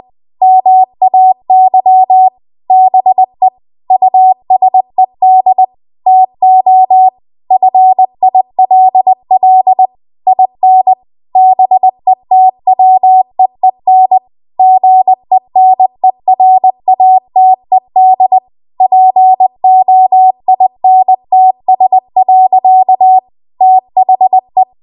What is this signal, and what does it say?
Morse code, heard as a keyed tone, about 20 wpm, 750 Hz. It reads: MAY BE USED TO FILL IN BETWEEN GENERATED POINTS. THE
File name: Archive / 20 WPM / 20WPM